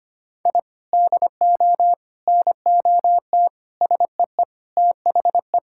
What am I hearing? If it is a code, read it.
Morse code, heard as a keyed tone, about 25 wpm, 700 Hz. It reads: I DO NOT SEE THE
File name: practice / 25wpm / Sentences with 4-Word Phrases - 25wpm